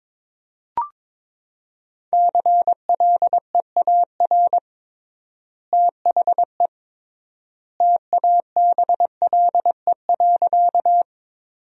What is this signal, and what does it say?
Morse code, heard as a keyed tone, about 22 wpm, 700 Hz. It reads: CLEAR THE TABLE.
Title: 3x Word-Spacing - Sentences from Top 500 Words - 22wpm